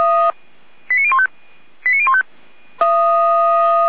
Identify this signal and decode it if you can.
Five-tone selective call (ZVEI), heard as quick runs of five tones, tones of about 70 ms. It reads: 89015 89015